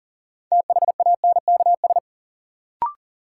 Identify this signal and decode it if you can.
Morse code, heard as a keyed tone, about 38 wpm, 700 Hz. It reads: THANKS
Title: Sentences from Top 600 Words - New Word Drill - 40wpm